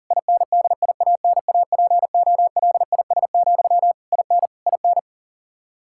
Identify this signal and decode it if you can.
Morse code, heard as a keyed tone, about 37 wpm, 700 Hz. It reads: INDIANAPOLIS, IN IN
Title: Practice QSO - Exchange 1 - Beginning - 40wpm